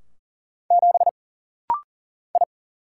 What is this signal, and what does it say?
Morse code, heard as a keyed tone, about 40 wpm, 700 Hz. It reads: 7 I
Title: Single Letter-Number - Rapid-Fire 40wpm